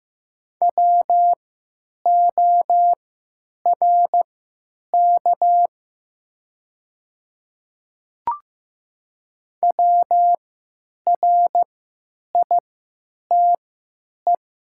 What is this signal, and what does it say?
Morse code, heard as a keyed tone, about 15 wpm, spaced as for 9 wpm, 700 Hz. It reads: WORK WRITE
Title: 3x Character Spacing - Top 200 Words - 15wpm